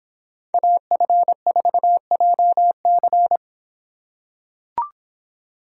Morse code, heard as a keyed tone, about 26 wpm, 700 Hz. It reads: AF4JC